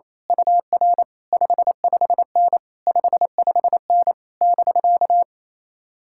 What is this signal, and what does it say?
Morse code, heard as a keyed tone, about 27 wpm, 700 Hz. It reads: UR 55N 55N <BK>
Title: Speed-Racing Common QSO Phrases - 28wpm